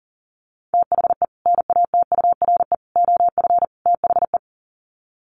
Morse code, heard as a keyed tone, about 37 wpm, 700 Hz. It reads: THE NATURE OF THE
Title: Sets of 4 Words 40wpm